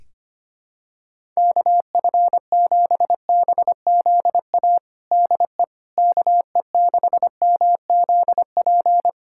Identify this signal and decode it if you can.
Morse code, heard as a keyed tone, about 25 wpm, 700 Hz. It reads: KF7BZA DE KE6MZP
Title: USA Callsign DE Callsign - 6 Characters - 25wpm